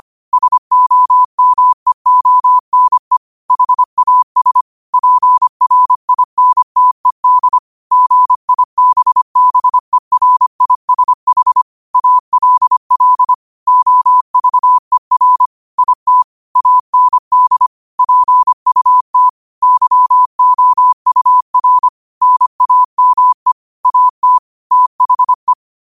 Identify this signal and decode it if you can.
Morse code, heard as a keyed tone, about 25 wpm, 1000 Hz. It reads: SOMEONE HAS PRINTED GIBBERISH ALL OVER IT AND PUT YOUR NAME AT THE
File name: qotdfiles / 25wpm